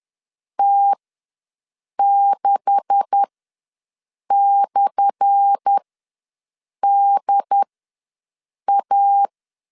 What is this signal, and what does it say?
Morse code, heard as a keyed tone, about 10 wpm, spaced as for 6 wpm, 800 Hz. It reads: T6/DA
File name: LEZIONI / 10 SIGNS / 10C DOT ws 10wpm 4213